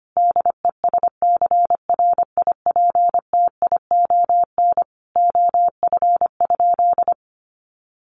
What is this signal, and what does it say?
Morse code, heard as a keyed tone, about 25 wpm, 700 Hz. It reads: DESCRIPTION OF?
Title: Sentences from Top 1300 Words 25wpm